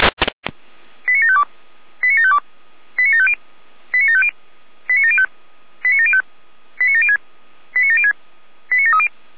Five-tone selective call (ZVEI), heard as quick runs of five tones, tones of about 70 ms. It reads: 89752 89752 89750 89750 89705 89705 89706 89706 89730
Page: /455/SDR/ZVEI/89490/